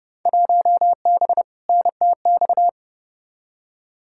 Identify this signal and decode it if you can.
Morse code, heard as a keyed tone, about 30 wpm, 700 Hz. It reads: 1B NTX
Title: Field Day Exchanges - 30wpm